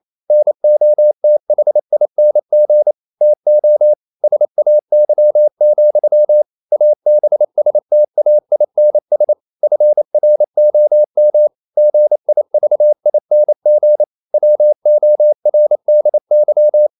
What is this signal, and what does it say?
Morse code, heard as a keyed tone, about 28 wpm, 600 Hz. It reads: NOTHING TO SAY, ABSTAINS FROM GIVING WORDY
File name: CW / Practice / EnglishQuotations0228